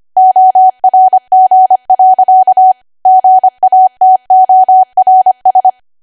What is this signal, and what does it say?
Morse code, heard as a keyed tone, about 25 wpm, 750 Hz. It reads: ORG. GATORS
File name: Archive / 25 WPM / 25WPM